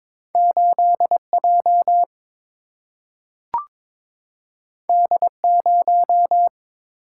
Morse code, heard as a keyed tone, about 22 wpm, 700 Hz. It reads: 8 J D 0
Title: Single-Letter Single-Number 22wpm